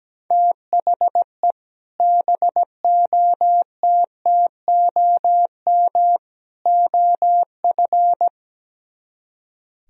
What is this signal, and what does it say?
Morse code, heard as a keyed tone, about 17 wpm, 700 Hz. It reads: THE BOTTOM OF
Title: Speed-Racing - Sets of 3 Words 17wpm